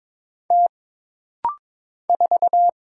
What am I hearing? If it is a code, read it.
Morse code, heard as a keyed tone, about 22 wpm, 700 Hz. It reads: T 4
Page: This rapid-fire timing is geared to push your ICR (Instant Character Recognition) skills.